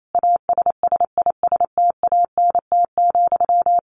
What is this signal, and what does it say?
Morse code, heard as a keyed tone, about 28 wpm, 700 Hz. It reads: ASSISTANT,
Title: A Sound Of Thunder 28wpm